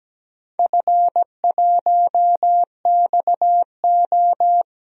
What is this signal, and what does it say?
Morse code, heard as a keyed tone, about 17 wpm, 700 Hz. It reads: F1XO
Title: DXCC-100 Call Signs - 4 Characters - 17wpm